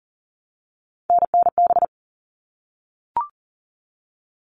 Morse code, heard as a keyed tone, about 40 wpm, 700 Hz. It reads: NNB
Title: All Cut Numbers - 3-Digit 40wpm